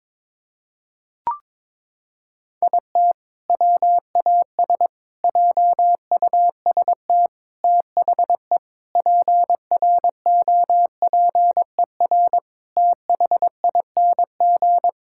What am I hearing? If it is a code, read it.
Morse code, heard as a keyed tone, about 22 wpm, 700 Hz. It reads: IT WAS JUST THE PROPER THING
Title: Sentences from Top 1000 Words - Review Part 5 - 22wpm